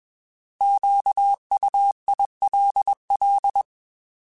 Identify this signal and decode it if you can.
Morse code, heard as a keyed tone, about 21 wpm, 800 Hz. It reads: QUILL
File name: Alphabetical / Alphabetical mp3s / Qs @20